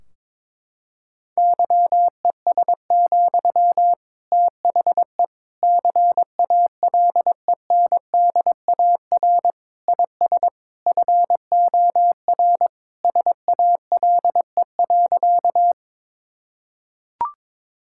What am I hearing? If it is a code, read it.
Morse code, heard as a keyed tone, about 22 wpm, 700 Hz. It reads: YES, THE CALENDAR IS FOR SALE.
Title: Sentences from Top 1600 Words 22wpm